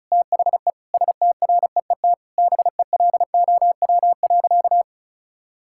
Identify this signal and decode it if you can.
Morse code, heard as a keyed tone, about 36 wpm, 700 Hz. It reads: THE STREET BELOW.
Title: Sentences from Top 1000 Words - Review Part 5 - 35wpm